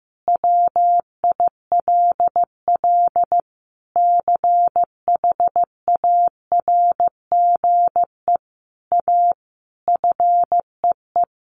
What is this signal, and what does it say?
Morse code, heard as a keyed tone, about 15 wpm, 700 Hz. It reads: WILL CHARGE A FEE
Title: Sentences from Top 1500 Words 15wpm